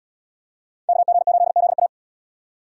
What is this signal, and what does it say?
Morse code, heard as a keyed tone, about 70 wpm, 700 Hz. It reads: KG8ZA